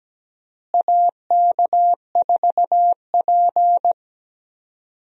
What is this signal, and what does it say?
Morse code, heard as a keyed tone, about 17 wpm, 700 Hz. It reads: AK4P
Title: USA Call Signs - 4 Characters - Repeated 2x - 17wpm